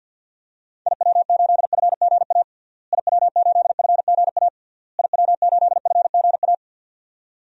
Morse code, heard as a keyed tone, about 47 wpm, 700 Hz. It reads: IW8FGA IW8FGA IW8FGA
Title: DXCC-100 Call Signs - 6 Characters - Repeated 3x - 50wpm